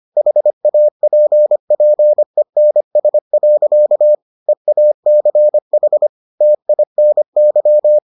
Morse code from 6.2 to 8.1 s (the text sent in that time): TINY